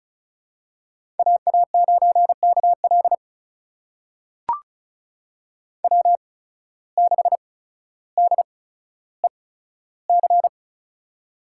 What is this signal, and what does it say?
Morse code, heard as a keyed tone, about 36 wpm, spaced as for 15 wpm, 700 Hz. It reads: AA9KL W6DEC
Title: USA Call Signs - 5 Characters - 8x Character Spacing - 35wpm